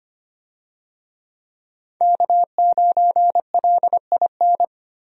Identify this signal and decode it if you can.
Morse code, heard as a keyed tone, about 25 wpm, 700 Hz. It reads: K9LIN
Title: USA Call Signs - 5 Characters - Repeated 3x - 25wpm